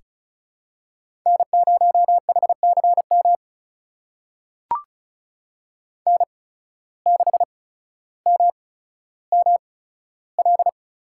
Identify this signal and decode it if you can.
Morse code, heard as a keyed tone, about 36 wpm, spaced as for 15 wpm, 700 Hz. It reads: N0HCM N6MML
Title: USA Call Signs - 5 Characters - 8x Character Spacing - 35wpm